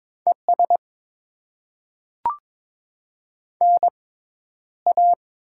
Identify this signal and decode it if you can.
Morse code, heard as a keyed tone, about 22 wpm, 690 Hz. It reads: E S N A
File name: Top 20 QSO Elements - 6x Character Spacing - 22wpm